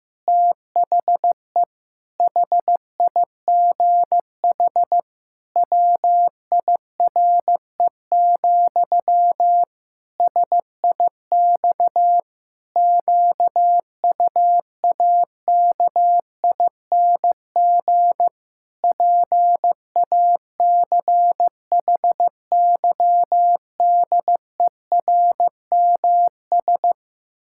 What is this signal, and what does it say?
Morse code, heard as a keyed tone, about 15 wpm, 700 Hz. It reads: THE HIGH WIRE, SIX QUAKING PACHYDERMS
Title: Pangram Challenge v2 - 15wpm